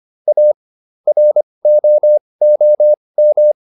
Morse code, heard as a keyed tone, about 25 wpm, spaced as for 20 wpm, 600 Hz. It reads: A ROOM